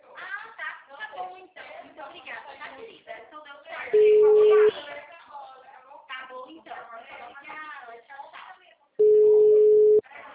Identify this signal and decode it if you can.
Ringback tone (Europe, 425 Hz), with speech.